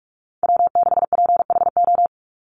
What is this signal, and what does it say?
Morse code, heard as a keyed tone, about 44 wpm, 700 Hz. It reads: W6PHO
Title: Call Signs Repeated 3x - 45wpm